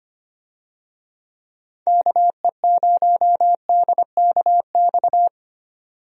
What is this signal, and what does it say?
Morse code, heard as a keyed tone, about 25 wpm, 700 Hz. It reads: KE0DKX
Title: USA Call Signs - 6 Characters - 25wpm